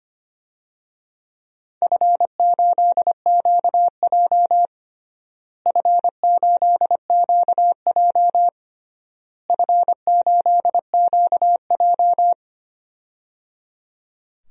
Morse code, heard as a keyed tone, about 25 wpm, 700 Hz. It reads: F8QJ F8QJ F8QJ
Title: DXCC-100 Call Signs - 4 Characters - Repeated 3x - 25wpm